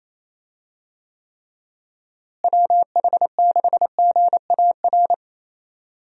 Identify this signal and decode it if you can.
Morse code, heard as a keyed tone, about 28 wpm, 700 Hz. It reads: WH6GAR
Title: USA Call Signs - 6 Characters - Repeated 3x - 28wpm